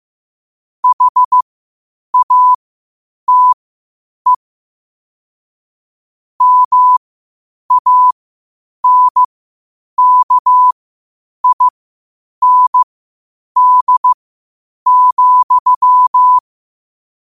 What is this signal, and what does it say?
Morse code, heard as a keyed tone, about 15 wpm, spaced as for 8 wpm, 1000 Hz. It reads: HATE MANKIND,
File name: qotdfiles / 8wpm